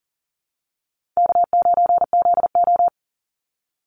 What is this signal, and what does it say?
Morse code, heard as a keyed tone, about 40 wpm, 700 Hz. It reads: K9ZO
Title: Easy Call Signs Repeated 2x - 40wpm